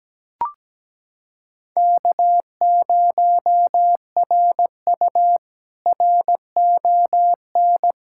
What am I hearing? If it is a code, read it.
Morse code, heard as a keyed tone, about 17 wpm, 700 Hz. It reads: K0RU RON
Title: K1USN SST QSO Practice - 17wpm